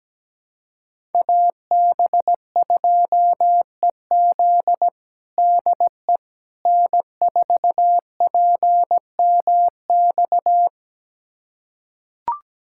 Morse code, heard as a keyed tone, about 17 wpm, 700 Hz. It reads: AB2EZ DE N4PMX
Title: USA Callsign DE Callsign - 5 Characters - 17wpm